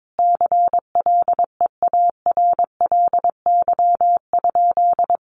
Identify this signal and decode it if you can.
Morse code, heard as a keyed tone, about 22 wpm, 700 Hz. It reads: CLEARLY?
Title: Sentences from Top 1300 Words 22wpm